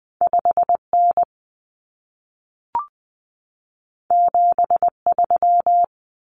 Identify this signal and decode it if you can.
Morse code, heard as a keyed tone, about 20 wpm, 700 Hz. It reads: 5N 73
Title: Common Cut Numbers - 2-Digit 20wpm